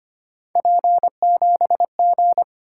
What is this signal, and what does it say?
Morse code, heard as a keyed tone, about 25 wpm, 700 Hz. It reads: P7G
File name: Three Letter-Number Combinations - 25wpm